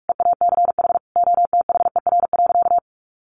Morse code, heard as a keyed tone, about 44 wpm, 700 Hz. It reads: EACH OTHER.
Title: Sentences from Top 1300 Words 45wpm